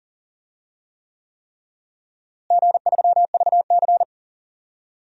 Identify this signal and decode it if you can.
Morse code, heard as a keyed tone, about 40 wpm, 700 Hz. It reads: G3VC
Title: DXCC-100 Call Signs - 4 Characters - Repeated 3x - 40wpm